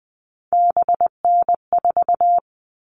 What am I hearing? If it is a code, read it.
Morse code, heard as a keyed tone, about 20 wpm, 700 Hz. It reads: BN4
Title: All Cut Numbers - 3-Digit 20wpm